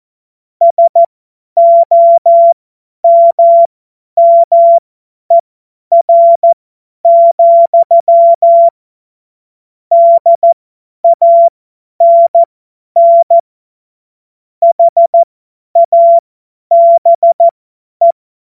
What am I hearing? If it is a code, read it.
Morse code, heard as a keyed tone, about 14 wpm, spaced as for 10 wpm, 680 Hz. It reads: SOMMER, DANN HABE
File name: Morsekurs / Witze / Witz 12 März 2024 50 F70